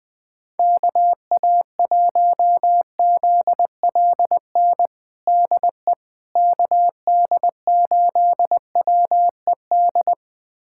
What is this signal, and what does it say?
Morse code, heard as a keyed tone, about 20 wpm, 700 Hz. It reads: KA1ZLN DE KD8WED
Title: USA Callsign DE Callsign - 6 Characters - 20wpm